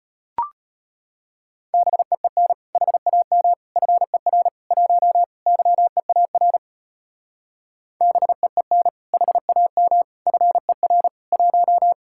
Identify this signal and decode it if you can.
Morse code, heard as a keyed tone, about 30 wpm, 700 Hz. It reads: BEEN HAM FER 1 YEAR BEEN HAM FER 1
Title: Speed-Racing Common QSO Phrases - 25wpm